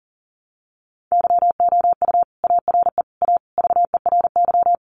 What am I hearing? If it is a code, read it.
Morse code, heard as a keyed tone, about 40 wpm, 700 Hz. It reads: YOU ARE A VERY
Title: Sentences from Top 1100 Words 40wpm